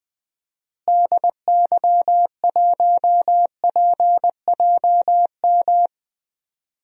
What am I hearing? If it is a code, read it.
Morse code, heard as a keyed tone, about 20 wpm, 700 Hz. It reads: DY1PJM
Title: DXCC-100 Call Signs - 6 Characters - Repeated 3x - 20wpm